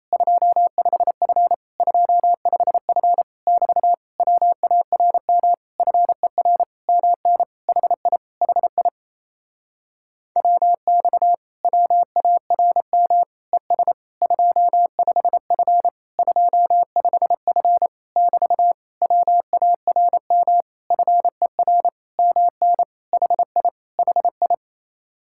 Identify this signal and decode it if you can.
Morse code, heard as a keyed tone, about 30 wpm, 700 Hz. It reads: 25F 25F <BT> WARM FER MN HI HI WX WARM ES 25F 25F <BT> WARM FER MN HI HI
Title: Speed-Racing Common QSO Phrases Encore - 28wpm